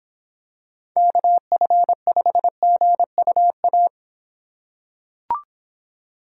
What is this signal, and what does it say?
Morse code, heard as a keyed tone, about 26 wpm, 700 Hz. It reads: KF5GUA